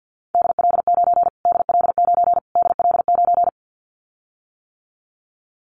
Morse code, heard as a keyed tone, about 50 wpm, 700 Hz. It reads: DL9 DL9 DL9
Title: Callsign Prefixes for Top 25 DXCC - Repeated 3x - 50wpm